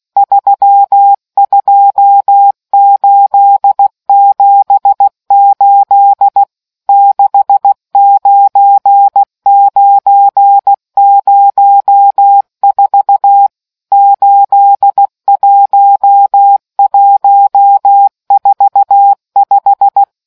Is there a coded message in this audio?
Yes: Morse code from 0.2 to 20.0 s, heard as a keyed tone, about 16 wpm, 800 Hz. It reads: 32878 69904 81145